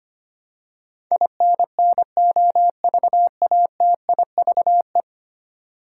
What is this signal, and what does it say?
Morse code, heard as a keyed tone, about 25 wpm, 700 Hz. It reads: INNOVATIVE
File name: 10 Letter Words - First 3 Letters Given - 25wpm